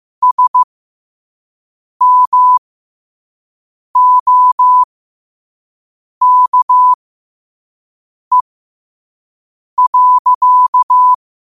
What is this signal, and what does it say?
Morse code, heard as a keyed tone, about 15 wpm, spaced as for 5 wpm, 1000 Hz. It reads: SMOKE.
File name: qotdfiles / 5wpm